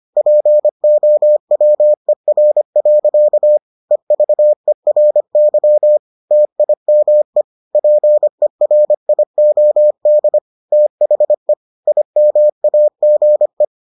Morse code, heard as a keyed tone, about 25 wpm, 600 Hz. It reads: POWER. EVERY TIME PERIOD THE IMAGE